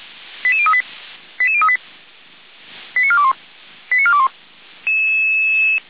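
Five-tone selective call (ZVEI), heard as quick runs of five tones, tones of about 70 ms. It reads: 88038 88038 89421 89421
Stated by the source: ZVEI five-tone call received on SDR.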